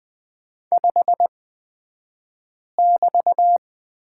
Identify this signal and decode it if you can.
Morse code, heard as a keyed tone, about 20 wpm, spaced as for 5 wpm, 700 Hz. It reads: 5<BT>
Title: 223 Lesson 36d - BT - Two Characters 20wpm